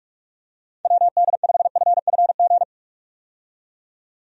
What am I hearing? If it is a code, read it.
Morse code, heard as a keyed tone, about 46 wpm, 700 Hz. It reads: WD5FFG
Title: USA Call Signs - 6 Characters - 45wpm